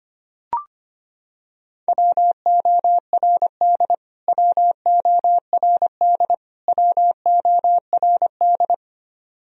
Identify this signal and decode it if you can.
Morse code, heard as a keyed tone, about 25 wpm, 700 Hz. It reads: WORD WORD WORD
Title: Top 100 Words Repeated 25wpm